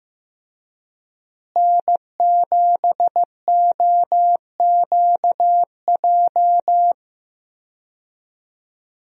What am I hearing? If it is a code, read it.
Morse code, heard as a keyed tone, about 15 wpm, 700 Hz. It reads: N7OQJ
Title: USA Call Signs - 5 Characters - Repeated 3x - 15wpm